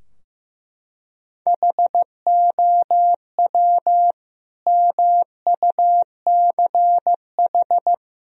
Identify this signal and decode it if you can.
Morse code, heard as a keyed tone, about 15 wpm, 700 Hz. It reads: HOW MUCH
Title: Sentences from Top 1000 Words - Review Part 3 - 15wpm